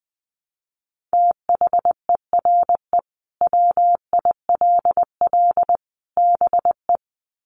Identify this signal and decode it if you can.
Morse code, heard as a keyed tone, about 20 wpm, 700 Hz. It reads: THERE WILL BE
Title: Sets of 3 Words 20wpm